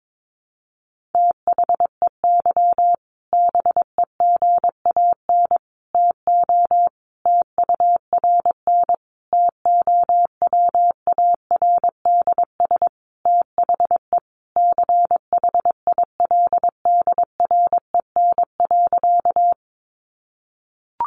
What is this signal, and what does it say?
Morse code, heard as a keyed tone, about 22 wpm, 700 Hz. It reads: THEY BEGAN TO TURN TOWARDS THE CHILDREN.
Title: Sentences from Top 1100 Words 22wpm